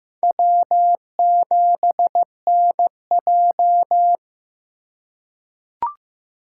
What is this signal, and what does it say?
Morse code, heard as a keyed tone, about 15 wpm, 700 Hz. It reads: W7NJ
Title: USA Call Signs - 4 Characters - 4x Character Spacing - 15wpm